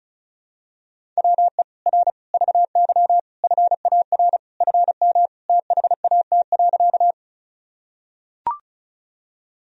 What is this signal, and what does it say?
Morse code, heard as a keyed tone, about 36 wpm, 700 Hz. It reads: WE R VY FAR FM THAT.
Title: Sentences from Top 300 Words and Abbreviations 35wpm